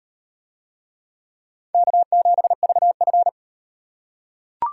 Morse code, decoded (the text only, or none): K7VF